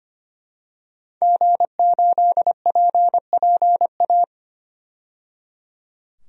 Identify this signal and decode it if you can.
Morse code, heard as a keyed tone, about 25 wpm, 700 Hz. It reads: G8PPA
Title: DXCC-100 Call Signs - 5 Characters - 25wpm